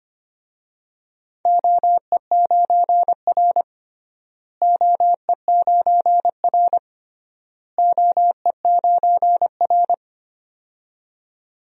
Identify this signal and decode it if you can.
Morse code, heard as a keyed tone, about 25 wpm, 700 Hz. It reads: OE9R OE9R OE9R
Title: DXCC-100 Call Signs - 4 Characters - Repeated 3x - 25wpm